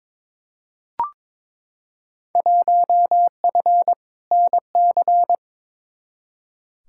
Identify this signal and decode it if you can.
Morse code, heard as a keyed tone, about 22 wpm, 700 Hz. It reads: 1F NC
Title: Field Day Exchanges - 22wpm